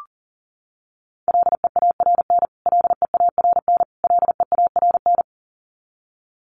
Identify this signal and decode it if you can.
Morse code, heard as a keyed tone, about 40 wpm, 700 Hz. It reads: LEARN LEARN LEARN
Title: Top 500 Words Repeated 40wpm